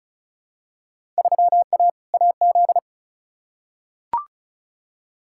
Morse code, heard as a keyed tone, about 36 wpm, 700 Hz. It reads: 3A AZ
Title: Field Day Exchanges - 35wpm